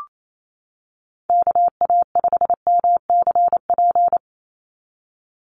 Morse code, heard as a keyed tone, about 28 wpm, 700 Hz. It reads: KA5MCP
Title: USA Call Signs 28wpm